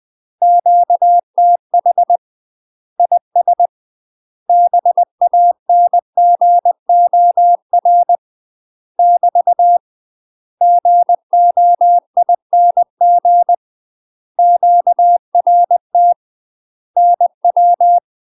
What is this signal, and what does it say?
Morse code, heard as a keyed tone, about 20 wpm, 700 Hz. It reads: QTH IS BANGOR <BT> GOING QRT NW
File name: UK MASTER QSOs TRAINING/20WPM